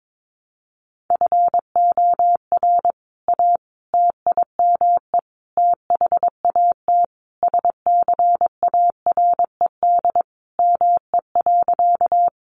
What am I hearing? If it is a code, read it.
Morse code, heard as a keyed tone, about 22 wpm, 700 Hz. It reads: FOR A TIME THAT SCARED ME.
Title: Sentences from Top 700 Words 22wpm